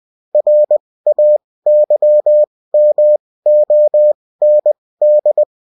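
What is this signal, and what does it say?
Morse code, heard as a keyed tone, about 20 wpm, spaced as for 16 wpm, 600 Hz. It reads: RAYMOND
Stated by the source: Bulletin in text and CW form.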